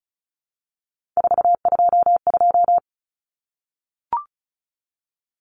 Morse code, heard as a keyed tone, about 34 wpm, 700 Hz. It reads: 422
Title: Numbers - 3-Digit 35wpm